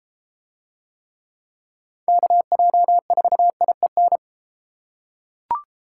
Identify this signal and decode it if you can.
Morse code, heard as a keyed tone, about 30 wpm, 700 Hz. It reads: KJ4IEN